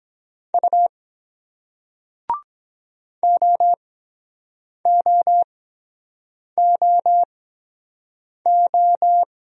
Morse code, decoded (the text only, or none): U OOOO